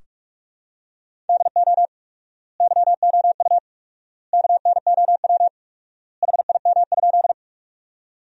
Morse code, heard as a keyed tone, about 46 wpm, 700 Hz. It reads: DO YOU KNOW HIM?
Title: 4x Word-Spacing - Sentences from Top 100 Words - 45wpm